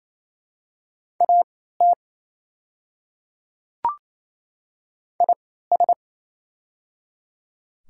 Morse code, heard as a keyed tone, about 28 wpm, spaced as for 16 wpm, 700 Hz. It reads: AT IS
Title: 3x Character Spacing - Top 100 Words - 28wpm